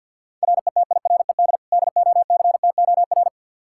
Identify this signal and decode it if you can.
Morse code, heard as a keyed tone, about 52 wpm, 700 Hz. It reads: RETIRED DOCTOR
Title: Practice QSO - Exchange 3 - More Info - 50wpm